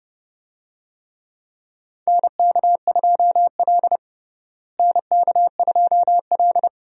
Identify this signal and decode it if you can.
Morse code, heard as a keyed tone, about 30 wpm, 700 Hz. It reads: NK2L NK2L
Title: USA Call Signs - 4 Characters - Repeated 3x - 30wpm